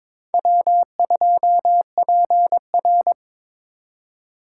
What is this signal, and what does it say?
Morse code, heard as a keyed tone, about 26 wpm, 700 Hz. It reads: W2PR
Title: Speed Racing - USA 1x2 Call Signs 22wpm